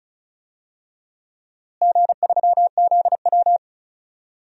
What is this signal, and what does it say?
Morse code, heard as a keyed tone, about 36 wpm, 700 Hz. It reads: G3ZW
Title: DXCC-100 Call Signs - 4 Characters - Repeated 3x - 35wpm